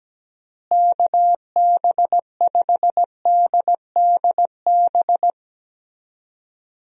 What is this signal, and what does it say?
Morse code, heard as a keyed tone, about 17 wpm, 700 Hz. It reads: KB5DDB
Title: USA Call Signs - 6 Characters - Repeated 2x - 17wpm